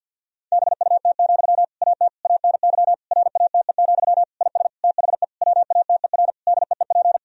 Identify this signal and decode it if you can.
Morse code, heard as a keyed tone, about 50 wpm, 700 Hz. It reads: BUT, AT ANY RATE, IS THE WATER DEEP
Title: Sentences from Top 1000 Words - Review Part 4 - 50wpm